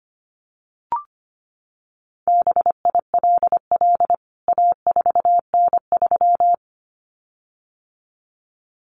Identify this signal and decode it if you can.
Morse code, heard as a keyed tone, about 25 wpm, 700 Hz. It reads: BILL A4N3
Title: CWT Exchange 25wpm